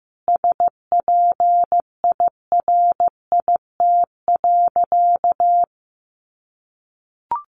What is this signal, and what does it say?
Morse code, heard as a keyed tone, about 15 wpm, 700 Hz. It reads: SPIRIT.
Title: Sentences from Top 1400 Words 15wpm